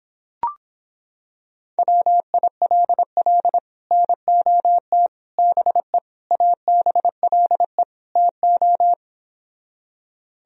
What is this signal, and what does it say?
Morse code, heard as a keyed tone, about 26 wpm, 700 Hz. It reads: WILL NOT BE ABLE TO